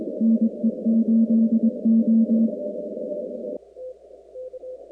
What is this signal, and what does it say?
Morse code, heard as a keyed tone, about 22 wpm, 230 Hz. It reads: NE8O
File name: contests / 2019-cw-wpx-kt5j-k5pi- -2330 0-45